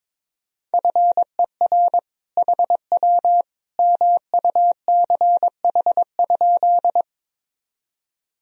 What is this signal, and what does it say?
Morse code, heard as a keyed tone, about 21 wpm, 700 Hz. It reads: FER HW MUCH?
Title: Sentences from Top 200 Words and Abbreviations 22wpm